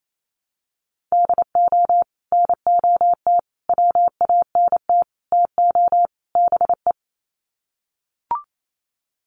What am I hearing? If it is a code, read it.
Morse code, heard as a keyed tone, about 28 wpm, 700 Hz. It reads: DO NOT WANT TO BE
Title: Sets of 5 Words 28wpm